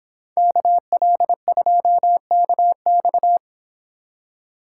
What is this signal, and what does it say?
Morse code, heard as a keyed tone, about 25 wpm, 700 Hz. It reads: KL2KX